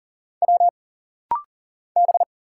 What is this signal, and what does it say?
Morse code, heard as a keyed tone, about 40 wpm, 700 Hz. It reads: W B
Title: Single Letter-Number Alt - Rapid-Fire 40wpm